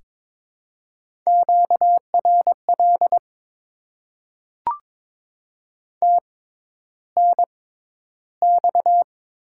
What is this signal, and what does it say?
Morse code, heard as a keyed tone, about 22 wpm, spaced as for 11 wpm, 700 Hz. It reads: QRL TNX
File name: Top 20 QSO Elements - 6x Character Spacing - 22wpm